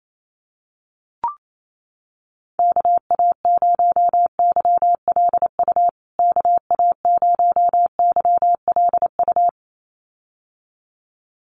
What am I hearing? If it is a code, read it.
Morse code, heard as a keyed tone, about 28 wpm, 700 Hz. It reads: KA0YLU KA0YLU
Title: USA Call Signs Repeated 2x 28wpm